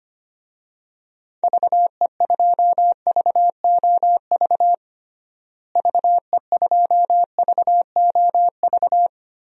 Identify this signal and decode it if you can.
Morse code, heard as a keyed tone, about 25 wpm, 700 Hz. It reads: VE2VOV VE2VOV
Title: DXCC-100 Call Signs - 6 Characters - Repeated 3x - 25wpm